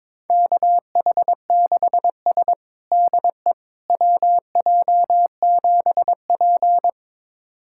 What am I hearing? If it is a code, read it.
Morse code, heard as a keyed tone, about 22 wpm, 700 Hz. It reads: KH6S DE WJ7P
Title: USA Callsign DE Callsign - 4 Characters - 22wpm